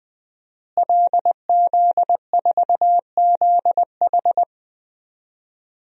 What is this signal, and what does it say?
Morse code, heard as a keyed tone, about 20 wpm, 700 Hz. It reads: LZ4ZH
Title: DXCC-100 Call Signs - 5 Characters - Repeated 3x - 20wpm